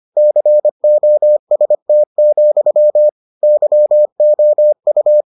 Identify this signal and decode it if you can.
Morse code, heard as a keyed tone, about 25 wpm, 600 Hz. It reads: COST, YOU